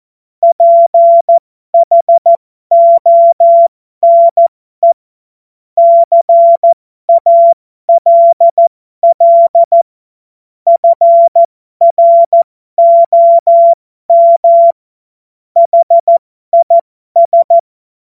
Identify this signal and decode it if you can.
Morse code, heard as a keyed tone, about 14 wpm, spaced as for 12 wpm, 680 Hz. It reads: PHONE CALL FROM HIS
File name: Morsekurs / Witze